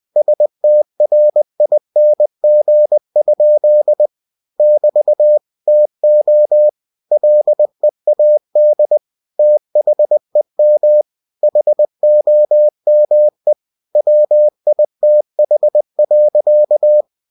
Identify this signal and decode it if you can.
Morse code, heard as a keyed tone, about 20 wpm, 600 Hz. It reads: STRING? <BT> TO LEAD THEM HOME WITH.
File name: cw-books / winnie-the-pooh / winnie-20wpm-page-0038